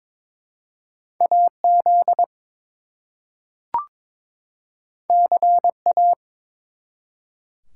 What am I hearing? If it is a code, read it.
Morse code, heard as a keyed tone, about 22 wpm, 700 Hz. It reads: AZ CA
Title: US States and Canadian Provinces 22wpm